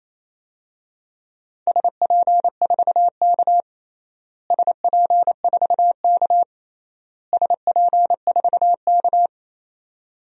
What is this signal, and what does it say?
Morse code, heard as a keyed tone, about 28 wpm, 700 Hz. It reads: SP4K SP4K SP4K
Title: DXCC-100 Call Signs - 4 Characters - Repeated 3x - 28wpm